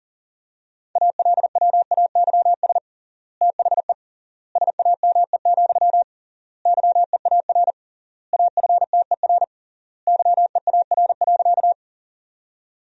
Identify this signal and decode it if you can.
Morse code, heard as a keyed tone, about 40 wpm, 700 Hz. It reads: ALWAYS THE SAME, YEAR AFTER YEAR.
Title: 3x Word-Spacing - Sentences from Top 300 Words - 40wpm